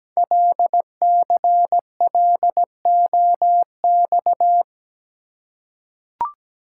Morse code, heard as a keyed tone, about 17 wpm, 700 Hz. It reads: LCLOX
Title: Code Groups - 5 Letter:Numbers - 17wpm